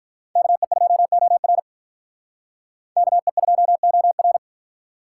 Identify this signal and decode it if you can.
Morse code, heard as a keyed tone, about 45 wpm, 700 Hz. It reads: KE2OR KE2OR